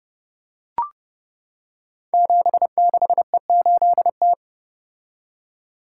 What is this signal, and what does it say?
Morse code, heard as a keyed tone, about 30 wpm, 700 Hz. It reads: 76E8T
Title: Code Groups - 5 Letter:Numbers - 30wpm